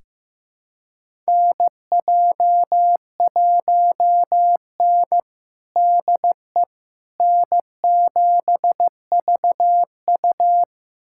Morse code, heard as a keyed tone, about 15 wpm, 700 Hz. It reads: NJ1N DE N7VU
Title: USA Callsign DE Callsign - 4 Characters - 15wpm